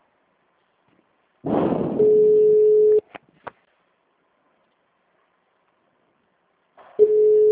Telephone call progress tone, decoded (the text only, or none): ringback tone